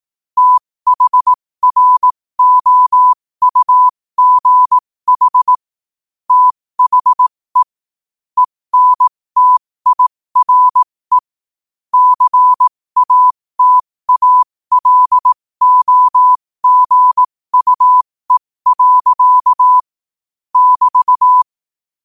Morse code, heard as a keyed tone, about 18 wpm, spaced as for 15 wpm, 1000 Hz. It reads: THROUGH THE ENTIRE CATALOGUE. <BT>
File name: qotdfiles / 15wpm